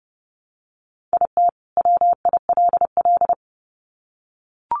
Morse code, heard as a keyed tone, about 30 wpm, 700 Hz. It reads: IT WILL
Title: Sets of 2 Words 30wpm